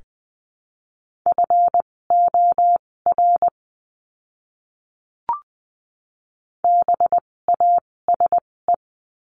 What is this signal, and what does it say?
Morse code, heard as a keyed tone, about 20 wpm, spaced as for 16 wpm, 700 Hz. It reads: FOR BASE
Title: 256 Lesson 42a - Words 12-20wpm